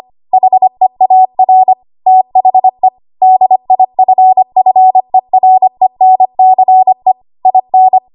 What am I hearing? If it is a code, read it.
Morse code, heard as a keyed tone, about 25 wpm, 750 Hz. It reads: HEAR THE DIFFERENCE IN
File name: Archive / 25 WPM / 25WPM